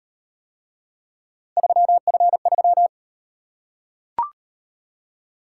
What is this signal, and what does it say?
Morse code, heard as a keyed tone, about 38 wpm, 700 Hz. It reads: 3F3